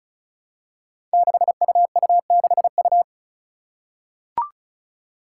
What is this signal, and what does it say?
Morse code, heard as a keyed tone, about 36 wpm, 700 Hz. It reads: 6UU6U
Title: Five Challenging Letter-Number Combinations 35wpm